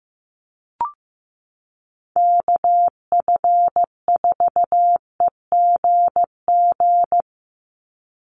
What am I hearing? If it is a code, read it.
Morse code, heard as a keyed tone, about 15 wpm, 700 Hz. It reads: KF4EGG
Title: USA Call Signs 15wpm